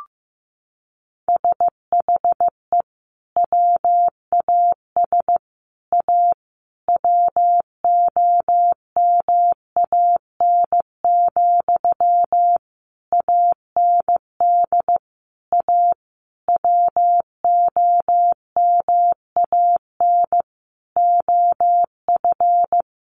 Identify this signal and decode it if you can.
Morse code, heard as a keyed tone, about 15 wpm, 700 Hz. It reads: SHE WAS A WOMAN, AND A WOMAN OF
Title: Sentences from Top 1400 Words 15wpm